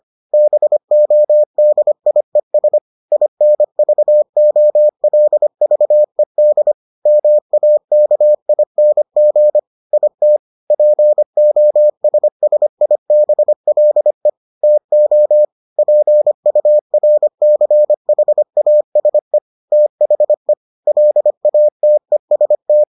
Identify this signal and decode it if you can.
Morse code, heard as a keyed tone, about 25 wpm, 600 Hz. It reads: BODIES INVOLVED MAKING IT POSSIBLE TO PURCHASE THE LATEST